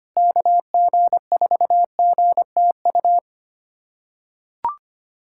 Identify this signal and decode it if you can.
Morse code, heard as a keyed tone, about 25 wpm, 700 Hz. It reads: KG4GTU
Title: USA Callsign DE Callsign - 6 Characters - 25wpm